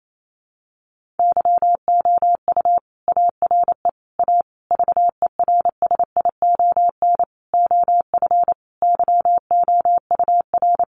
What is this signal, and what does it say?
Morse code, heard as a keyed tone, about 28 wpm, 700 Hz. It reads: YOU ARE A VERSION OF YOUR
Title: Sentences from Top 1400 Words 28wpm